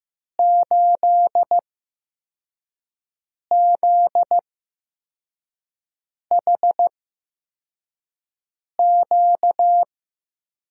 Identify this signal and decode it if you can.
Morse code, heard as a keyed tone, about 15 wpm, spaced as for 4 wpm, 700 Hz. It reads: 8ZHQ
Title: USA Call Signs - 5 Characters - 8x Character Spacing - 15wpm